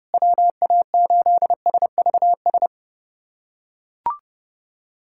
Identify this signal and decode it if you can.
Morse code, heard as a keyed tone, about 30 wpm, 700 Hz. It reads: WA8SVS
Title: USA Callsign DE Callsign 30wpm